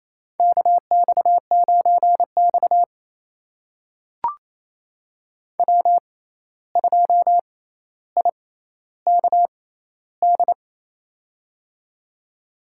Morse code, heard as a keyed tone, about 28 wpm, spaced as for 13 wpm, 700 Hz. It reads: KX9X W2IKD
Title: USA Call Signs - 6x Character Spacing - 28wpm